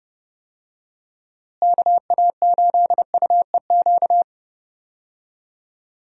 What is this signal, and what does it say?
Morse code, heard as a keyed tone, about 32 wpm, 700 Hz. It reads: KA8UEQ
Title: USA Call Signs - 6 Characters - Repeated 3x - 30wpm